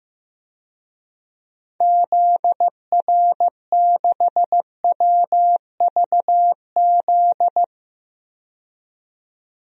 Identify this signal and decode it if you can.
Morse code, heard as a keyed tone, about 15 wpm, 700 Hz. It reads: ZR6WVZ
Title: DXCC-100 Call Signs - 6 Characters - 15wpm